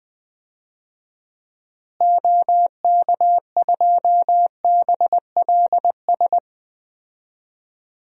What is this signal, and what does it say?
Morse code, heard as a keyed tone, about 19 wpm, 700 Hz. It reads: OK2BLS
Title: DXCC-100 Call Signs - 6 Characters - Repeated 2x - 20wpm